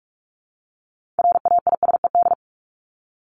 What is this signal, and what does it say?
Morse code, heard as a keyed tone, about 44 wpm, 700 Hz. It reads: RAISED
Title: ED Suffix - Top 100 Words with Base Words 45wpm